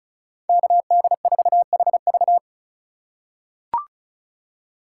Morse code, heard as a keyed tone, about 36 wpm, 700 Hz. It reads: KD4HV
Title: USA Call Signs - 5 Characters - 3x Character Spacing - 35wpm